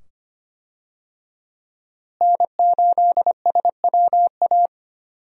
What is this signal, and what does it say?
Morse code, heard as a keyed tone, about 25 wpm, 700 Hz. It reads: N8SWA
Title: USA Call Signs - 5 Characters - Repeated 2x - 25wpm